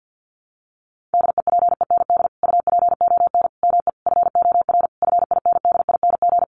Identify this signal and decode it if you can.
Morse code, heard as a keyed tone, about 50 wpm, 700 Hz. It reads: DEPEND UPON ME FOR FINDING
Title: Sentences from Top 1300 Words 50wpm